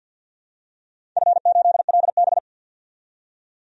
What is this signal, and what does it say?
Morse code, heard as a keyed tone, about 47 wpm, 700 Hz. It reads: F8LB
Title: DXCC-100 Call Signs - 4 Characters - 50wpm